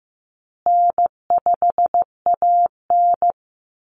Morse code, heard as a keyed tone, about 15 wpm, 700 Hz. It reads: N5AN
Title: Easy Call Signs Repeated 3x - 15wpm